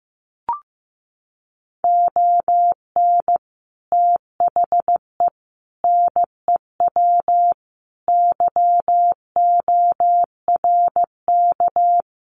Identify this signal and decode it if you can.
Morse code, heard as a keyed tone, about 15 wpm, 700 Hz. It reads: ON THE NEW YORK
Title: Sets of 5 Words Encore 15wpm